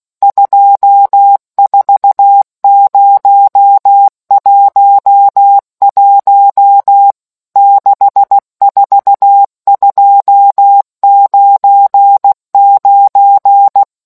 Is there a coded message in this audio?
Yes: Morse code from 0.2 to 13.8 s, heard as a keyed tone, about 16 wpm, 800 Hz. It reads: 24011 64299